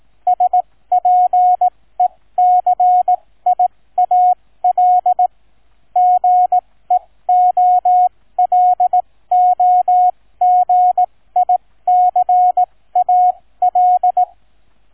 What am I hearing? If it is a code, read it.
Morse code, heard as a keyed tone, about 18 wpm, spaced as for 15 wpm, 730 Hz, with radio noise. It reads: SPECIAL GEOLOGICAL
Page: Shortened example of QNC Sound file